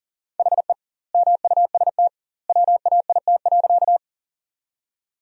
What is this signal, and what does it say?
Morse code, heard as a keyed tone, about 40 wpm, 700 Hz. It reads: HE MUST WAIT.
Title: 2x Word-Spacing - Sentences from Top 500 Words - 40wpm